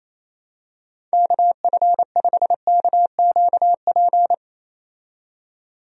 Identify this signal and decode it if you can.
Morse code, heard as a keyed tone, about 28 wpm, 700 Hz. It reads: KF5KQP
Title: USA Call Signs - 6 Characters - 28wpm